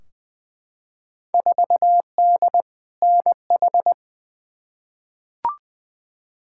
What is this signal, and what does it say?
Morse code, heard as a keyed tone, about 20 wpm, 700 Hz. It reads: 4D NH
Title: Field Day Exchanges Encore - 20wpm